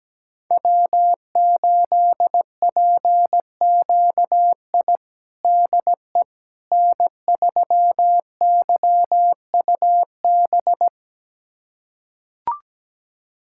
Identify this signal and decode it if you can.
Morse code, heard as a keyed tone, about 17 wpm, 700 Hz. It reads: W8PQI DE N3YUB
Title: USA Callsign DE Callsign - 5 Characters - 17wpm